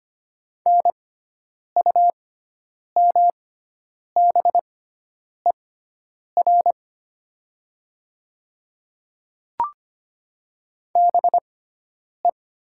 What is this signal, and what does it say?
Morse code, heard as a keyed tone, about 25 wpm, spaced as for 9 wpm, 700 Hz. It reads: NUMBER BE
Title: 6x Character Spacing - Top 100 Words - 25wpm